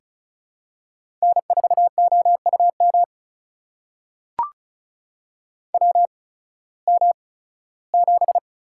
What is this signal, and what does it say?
Morse code, heard as a keyed tone, about 36 wpm, spaced as for 19 wpm, 700 Hz. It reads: N4OUM WM7
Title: USA Call Signs - 8x Character Spacing - 35wpm